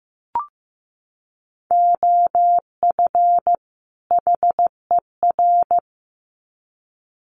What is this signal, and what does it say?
Morse code, heard as a keyed tone, about 15 wpm, 700 Hz. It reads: OF HER
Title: Sets of 2 Words 15wpm